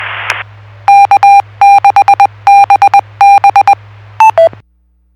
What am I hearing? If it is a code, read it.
Morse code, heard as a keyed tone, about 20 wpm, 800 Hz. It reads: K6BB